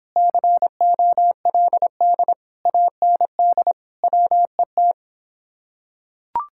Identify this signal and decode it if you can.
Morse code, heard as a keyed tone, about 26 wpm, 700 Hz. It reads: COLD AND WET